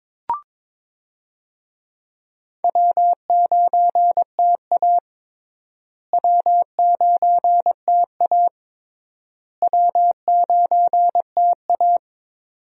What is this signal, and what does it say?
Morse code, heard as a keyed tone, about 22 wpm, 700 Hz. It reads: W9TA W9TA W9TA
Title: USA Call Signs - 4 Characters - Repeated 3x - 22wpm